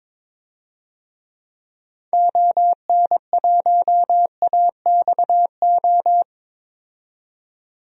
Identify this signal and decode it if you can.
Morse code, heard as a keyed tone, about 21 wpm, 700 Hz. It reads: ON1AXO
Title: DXCC-100 Call Signs - 6 Characters - 22wpm